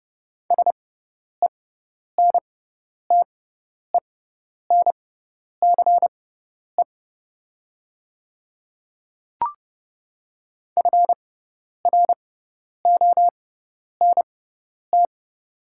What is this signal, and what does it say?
Morse code, heard as a keyed tone, about 31 wpm, spaced as for 10 wpm, 700 Hz. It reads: SENTENCE FRONT
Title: 6x Character Spacing - Top 500 Words - 30wpm